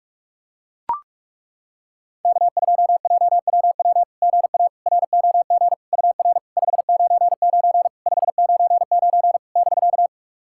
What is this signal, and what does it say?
Morse code, heard as a keyed tone, about 42 wpm, 690 Hz. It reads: K2JWW GA ROG UR 599 599 <BK>
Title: SOTA QSO Practice 45wpm